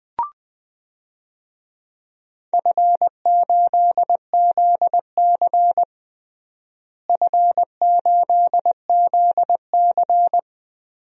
Morse code, heard as a keyed tone, about 21 wpm, 700 Hz. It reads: F8ZC F8ZC
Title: DXCC-100 Call Signs - 4 Characters - Repeated 3x - 20wpm